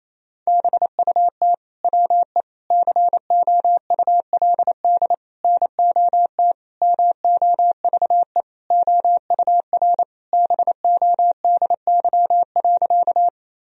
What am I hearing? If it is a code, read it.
Morse code, heard as a keyed tone, about 28 wpm, 700 Hz. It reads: BUT WE COULD NOT MOVE OUR BODY.
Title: Sentences from Top 1000 Words - Review Part 2 - 28wpm